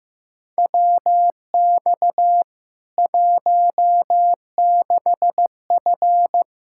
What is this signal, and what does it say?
Morse code, heard as a keyed tone, about 16 wpm, 700 Hz. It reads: WX 16F
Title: K1USN SST QSO Practice 15wpm